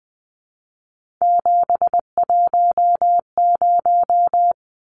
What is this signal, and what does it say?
Morse code, heard as a keyed tone, about 20 wpm, 700 Hz. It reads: 710
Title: Numbers - 3-Digit 20wpm